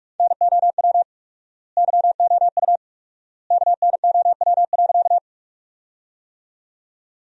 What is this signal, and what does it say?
Morse code, heard as a keyed tone, about 46 wpm, 700 Hz. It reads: NOW YOU KNOW.
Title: 4x Word-Spacing - Sentences from Top 100 Words - 45wpm